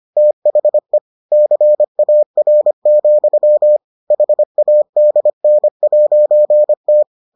Morse code, heard as a keyed tone, about 25 wpm, 600 Hz. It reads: THE CAR, HADN'T